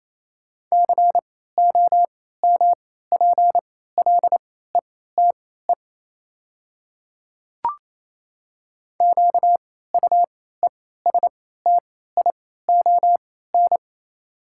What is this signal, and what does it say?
Morse code, heard as a keyed tone, about 28 wpm, spaced as for 16 wpm, 700 Hz. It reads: COMPLETE QUESTION
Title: 3x Character Spacing - Top 500 Words - 28wpm